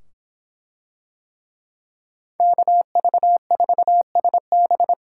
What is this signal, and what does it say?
Morse code, heard as a keyed tone, about 24 wpm, 700 Hz. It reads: KV4SB